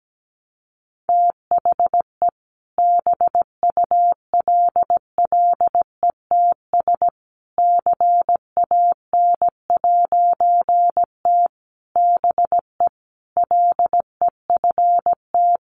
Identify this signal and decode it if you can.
Morse code, heard as a keyed tone, about 17 wpm, 700 Hz. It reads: THE BULLETS CAN'T BE LEFT
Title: A Sound Of Thunder 17wpm